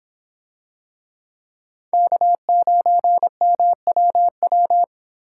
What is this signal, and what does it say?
Morse code, heard as a keyed tone, about 25 wpm, 700 Hz. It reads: K9MWW